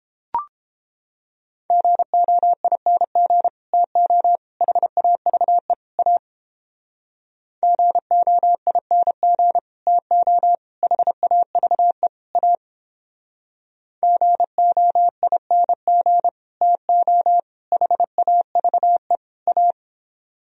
Morse code, heard as a keyed tone, about 29 wpm, 700 Hz. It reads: GOING TO HAVE A GOING TO HAVE A GOING TO HAVE A